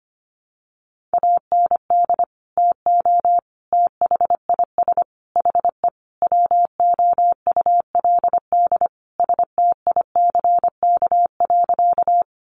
Morse code, heard as a keyed tone, about 25 wpm, 700 Hz. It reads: AND TO THIS HE WOULD STICK.
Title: Sentences from Top 1200 Words 25wpm